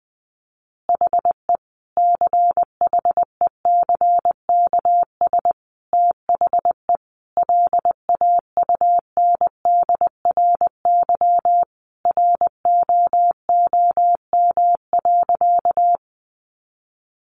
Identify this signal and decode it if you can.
Morse code, heard as a keyed tone, about 20 wpm, 700 Hz. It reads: HE CHECKS THE LAUNDRY ROOM.
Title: Easy Phrases 20wpm